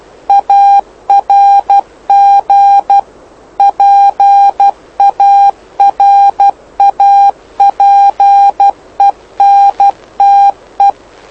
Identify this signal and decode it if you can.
Morse code, heard as a keyed tone, about 12 wpm, 800 Hz. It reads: ARG PARAPENTE
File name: lu4aao / lu7aa 09dic2011 exp planeador vuelo1